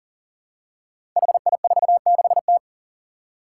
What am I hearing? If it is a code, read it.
Morse code, heard as a keyed tone, about 42 wpm, 700 Hz. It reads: HI46T
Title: Five Challenging Letter-Number Combinations 40wpm